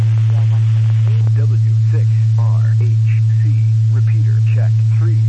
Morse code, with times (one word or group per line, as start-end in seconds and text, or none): none